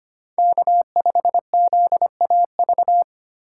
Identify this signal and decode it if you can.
Morse code, heard as a keyed tone, about 25 wpm, 700 Hz. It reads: K5ZAV
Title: USA Call Signs - 5 Characters - 4x Character Spacing - 25wpm